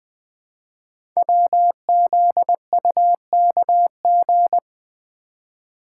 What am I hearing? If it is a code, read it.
Morse code, heard as a keyed tone, about 20 wpm, 700 Hz. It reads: WZUKG
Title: Code Groups - 5 Letters - 20wpm